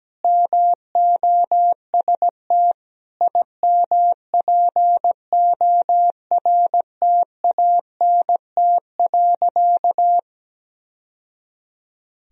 Morse code, heard as a keyed tone, about 17 wpm, 700 Hz. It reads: MOST IMPORTANT.
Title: Sentences from Top 1000 Words - Review Part 2 - 17wpm